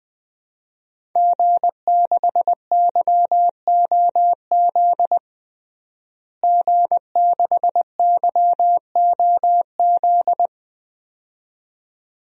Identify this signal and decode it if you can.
Morse code, heard as a keyed tone, about 20 wpm, 700 Hz. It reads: G6YOZ G6YOZ
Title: DXCC-100 Call Signs - 5 Characters - Repeated 2x - 20wpm